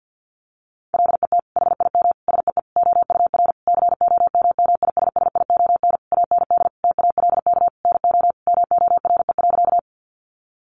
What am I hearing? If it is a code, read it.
Morse code, heard as a keyed tone, about 50 wpm, 700 Hz. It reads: LET HIM SEE OUR COMMISSION AND TALK NO MORE.
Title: Sentences from Top 1400 Words 50wpm